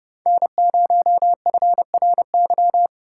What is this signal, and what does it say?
Morse code, heard as a keyed tone, about 29 wpm, 700 Hz. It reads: N0FRY